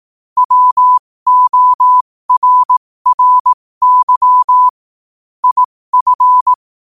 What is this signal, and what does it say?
Morse code, heard as a keyed tone, about 18 wpm, spaced as for 15 wpm, 1000 Hz. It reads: WORRY IF